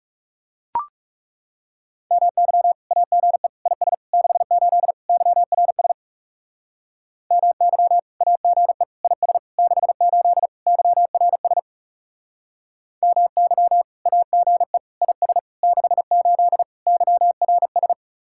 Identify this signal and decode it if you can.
Morse code, heard as a keyed tone, about 37 wpm, 700 Hz. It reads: MY AGE IS 68 YRS MY AGE IS 68 YRS MY AGE IS 68 YRS
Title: Speed-Racing Common QSO Phrases - 30wpm